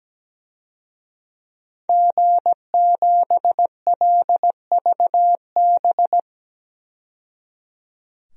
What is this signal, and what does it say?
Morse code, heard as a keyed tone, about 17 wpm, 700 Hz. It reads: G7LVB
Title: DXCC-100 Call Signs - 5 Characters - 17wpm